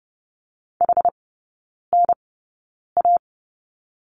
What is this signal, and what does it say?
Morse code, heard as a keyed tone, about 30 wpm, spaced as for 9 wpm, 700 Hz. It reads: HNA
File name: 081 Lesson 14e - Number 2 - Three Characters 30wpm